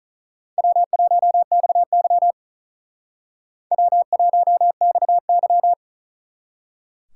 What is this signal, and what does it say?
Morse code, heard as a keyed tone, about 45 wpm, 700 Hz. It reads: W1XY W1XY